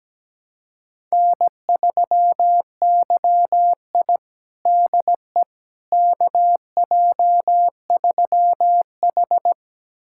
Morse code, heard as a keyed tone, about 17 wpm, 700 Hz. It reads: N3YI DE KJ3H
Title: USA Callsign DE Callsign - 4 Characters - 17wpm